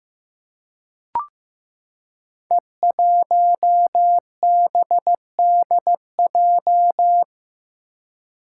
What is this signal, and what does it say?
Morse code, heard as a keyed tone, about 15 wpm, 700 Hz. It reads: E1BDJ
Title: Code Groups - 5 Letter:Numbers - 15wpm